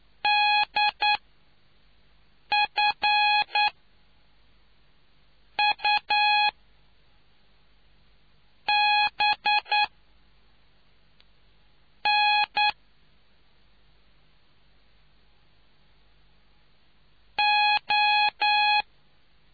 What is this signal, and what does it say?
Morse code, heard as a keyed tone, about 9 wpm, spaced as for 5 wpm, 830 Hz. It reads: DF U B N O